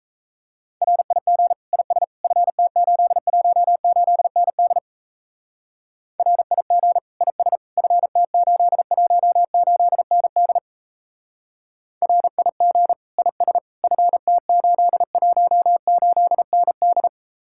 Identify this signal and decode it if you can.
Morse code, heard as a keyed tone, about 34 wpm, 700 Hz. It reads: RIG IS FT818ND RIG IS FT818ND RIG IS FT818ND
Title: Speed-Racing Common QSO Phrases - 28wpm